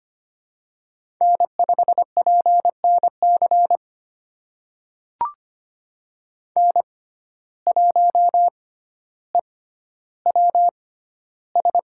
Morse code, heard as a keyed tone, about 25 wpm, spaced as for 13 wpm, 700 Hz. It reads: N5PNC N1EWS
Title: USA Call Signs - 5 Characters - 6x Character Spacing - 25wpm